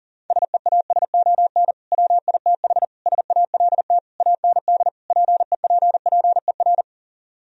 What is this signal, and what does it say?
Morse code, heard as a keyed tone, about 40 wpm, 700 Hz. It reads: SEASON WITH SALT AND PEPPER